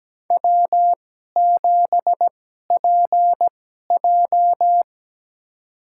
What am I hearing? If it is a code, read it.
Morse code, heard as a keyed tone, about 17 wpm, spaced as for 12 wpm, 700 Hz. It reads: W7PJ
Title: USA Call Signs - 4 Characters - 2x Character Spacing - 17wpm